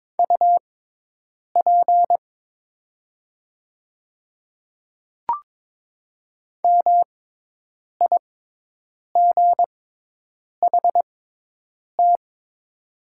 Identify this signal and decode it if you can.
Morse code, heard as a keyed tone, about 22 wpm, spaced as for 8 wpm, 690 Hz. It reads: UP MIGHT
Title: 6x Character Spacing - Top 300 Words - 22wpm